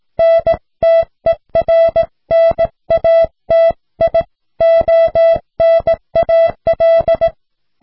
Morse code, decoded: NTERNATIONAL